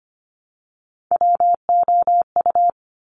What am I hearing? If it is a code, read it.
Morse code, heard as a keyed tone, about 25 wpm, 700 Hz. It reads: WOU
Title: Top Three Letter Combinations in Words 25wpm